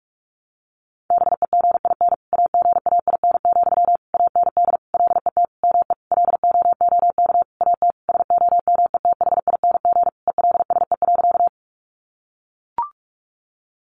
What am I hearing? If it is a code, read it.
Morse code, heard as a keyed tone, about 44 wpm, 700 Hz. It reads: BEGIN AGAIN, AND LET ME LOOK AT SOMETHING ELSE.
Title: Sentences from Top 800 Words 45wpm